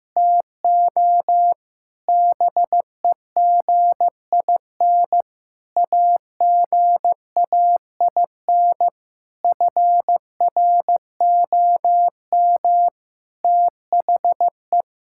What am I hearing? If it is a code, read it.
Morse code, heard as a keyed tone, about 15 wpm, 700 Hz. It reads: TO BEGIN AGAIN FROM THE
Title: Sentences from Top 1000 Words - Review Part 5 - 15wpm